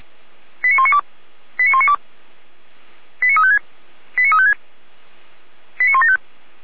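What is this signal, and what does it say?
Five-tone selective call (ZVEI), heard as quick runs of five tones, tones of about 70 ms. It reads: 89192 89192 89356 89356 89175
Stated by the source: ZVEI five-tone selective call received by SDR.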